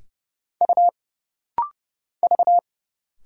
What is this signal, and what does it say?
Morse code, heard as a keyed tone, about 30 wpm, 700 Hz. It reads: U V
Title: Single Letters - Rapid-Fire 30wpm